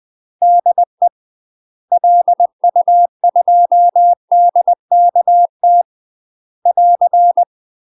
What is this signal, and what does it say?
Morse code, heard as a keyed tone, about 20 wpm, 700 Hz. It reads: DE LU2DKT <AR>
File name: UK MASTER QSOs TRAINING / 20WPM / UK MASTER QSOs 0032